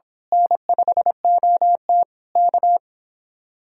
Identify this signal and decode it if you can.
Morse code, heard as a keyed tone, about 25 wpm, 700 Hz. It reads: N5OT K
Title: Speed-Racing Common QSO Phrases Encore - 22wpm